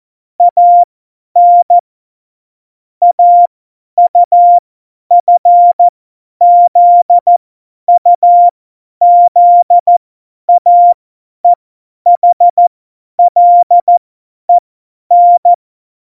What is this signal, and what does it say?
Morse code, heard as a keyed tone, about 14 wpm, spaced as for 10 wpm, 700 Hz. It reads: AN AUFZUZAEHLEN
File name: Morsekurs / Witze / Witz 19. Januar 2021, 50 F70